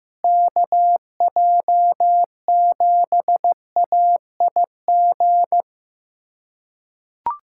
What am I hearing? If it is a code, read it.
Morse code, heard as a keyed tone, about 14 wpm, 700 Hz. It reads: KJ7AIG
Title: USA Callsign DE Callsign - 6 Characters - 15wpm